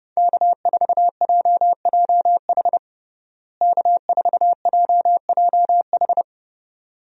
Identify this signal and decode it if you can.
Morse code, heard as a keyed tone, about 32 wpm, 700 Hz. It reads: K4JJH K4JJH
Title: USA Call Signs - 5 Characters - Repeated 2x - 30wpm